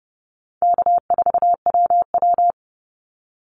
Morse code, heard as a keyed tone, about 30 wpm, 700 Hz. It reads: K4WW
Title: Call Signs - 30wpm